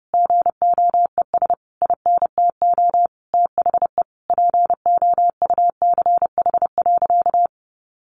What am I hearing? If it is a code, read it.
Morse code, heard as a keyed tone, about 30 wpm, 700 Hz. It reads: GOES INTO THE POUCH.
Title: Easy Phrases 30wpm